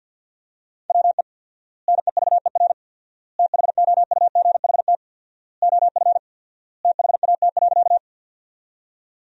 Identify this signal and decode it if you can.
Morse code, heard as a keyed tone, about 50 wpm, 700 Hz. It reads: WE NEVER THOUGHT OF THAT.
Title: 4x Word-Spacing - Sentences from Top 300 Words - 50wpm